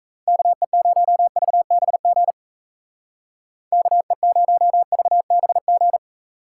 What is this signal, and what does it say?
Morse code, heard as a keyed tone, about 34 wpm, 700 Hz. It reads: KE0VBG KE0VBG
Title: Speed Racing - USA Call Signs 28wpm